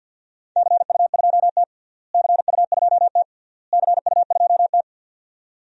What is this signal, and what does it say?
Morse code, heard as a keyed tone, about 50 wpm, 700 Hz. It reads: CU2T CU2T CU2T
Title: DXCC-100 Call Signs - 4 Characters - Repeated 3x - 50wpm